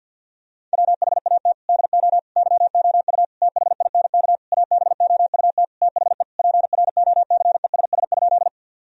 Morse code, heard as a keyed tone, about 52 wpm, 700 Hz. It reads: WHAT DO YOU THINK ABOUT THE PROCESS?
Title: Sentences from Top 1000 Words - Review Part 3 - 50wpm